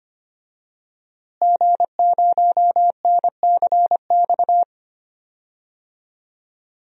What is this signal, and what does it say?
Morse code, heard as a keyed tone, about 25 wpm, 700 Hz. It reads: G0NCX
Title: DXCC-100 Call Signs - 5 Characters - Repeated 3x - 25wpm